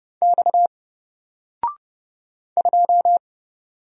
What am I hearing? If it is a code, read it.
Morse code, heard as a keyed tone, about 30 wpm, 700 Hz. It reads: X 2
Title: Single Letter-Number 30wpm